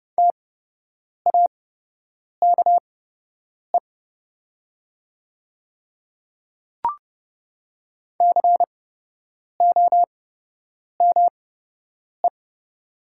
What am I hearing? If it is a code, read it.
Morse code, heard as a keyed tone, about 30 wpm, spaced as for 8 wpm, 700 Hz. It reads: TAKE COME
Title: 8x Character Spacing - Top 200 Words - 30wpm